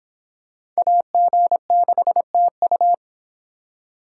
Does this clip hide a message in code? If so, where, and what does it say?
Yes: Morse code from 0.8 to 2.9 s, heard as a keyed tone, about 26 wpm, 700 Hz. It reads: AG6TU